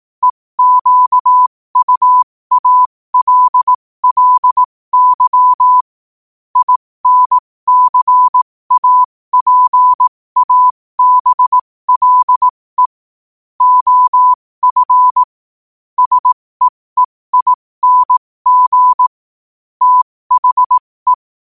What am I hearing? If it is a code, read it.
Morse code, heard as a keyed tone, about 18 wpm, spaced as for 15 wpm, 1000 Hz. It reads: EQUALLY INCAPABLE OF SEEING THE
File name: qotdfiles / 15wpm